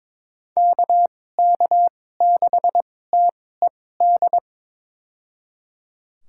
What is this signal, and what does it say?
Morse code, heard as a keyed tone, about 21 wpm, spaced as for 16 wpm, 700 Hz. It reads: KK6TED
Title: USA Call Signs - 6 Characters - 2x Character Spacing - 22wpm